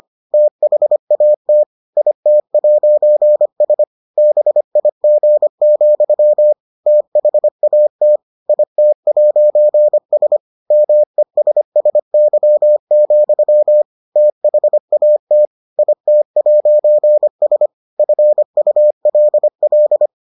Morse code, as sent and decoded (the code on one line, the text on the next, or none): - .... .- - / .. - .----. ... / -... .. --. --..-- / - .... .- - / .. - .----. ... / -- . ... ... -.-- --..-- / - .... .- - / .. - .----. ... / ..-. ..- .-.. .-..
THAT IT'S BIG, THAT IT'S MESSY, THAT IT'S FULL